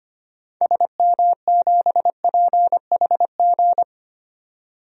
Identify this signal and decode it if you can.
Morse code, heard as a keyed tone, about 25 wpm, 700 Hz. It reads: SM7PHG
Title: DXCC-100 Call Signs - 6 Characters - 25wpm